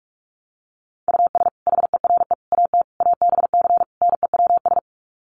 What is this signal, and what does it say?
Morse code, heard as a keyed tone, about 44 wpm, 700 Hz. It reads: US HERE AT ABC NEWS
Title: Sets of 5 Words 45wpm